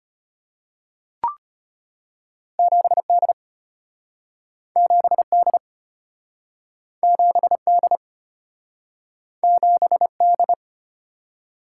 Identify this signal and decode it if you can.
Morse code, heard as a keyed tone, about 34 wpm, 700 Hz. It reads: 7D 7D 7D 7D